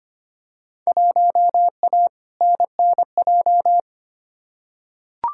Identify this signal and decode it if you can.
Morse code, heard as a keyed tone, about 25 wpm, 700 Hz. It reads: 1A NNJ
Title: Field Day Exchanges Encore - 25wpm